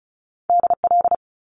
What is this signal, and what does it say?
Morse code, heard as a keyed tone, about 36 wpm, 700 Hz. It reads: DL
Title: Top Two Letter Combinations in Words 35wpm